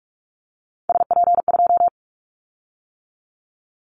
Morse code, heard as a keyed tone, about 44 wpm, 700 Hz. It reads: SP2
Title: Callsign Prefixes for Top 25 DXCC - 45wpm